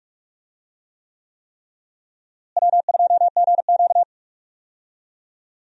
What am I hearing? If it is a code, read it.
Morse code, heard as a keyed tone, about 41 wpm, 700 Hz. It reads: W2GQ
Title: USA Call Signs - 4 Characters - Repeated 3x - 45wpm